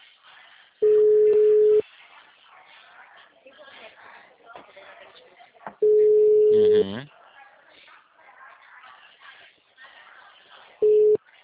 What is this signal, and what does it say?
Ringback tone (Europe, 425 Hz), with speech.